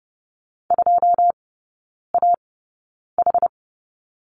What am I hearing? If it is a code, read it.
Morse code, heard as a keyed tone, about 30 wpm, spaced as for 9 wpm, 700 Hz. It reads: 2AH
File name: 081 Lesson 14e - Number 2 - Three Characters 30wpm